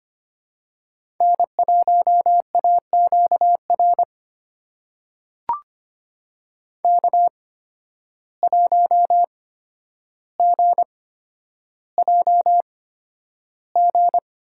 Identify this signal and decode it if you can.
Morse code, heard as a keyed tone, about 25 wpm, spaced as for 11 wpm, 700 Hz. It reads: N1AQR K1GJG
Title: USA Call Signs - 5 Characters - 8x Character Spacing - 25wpm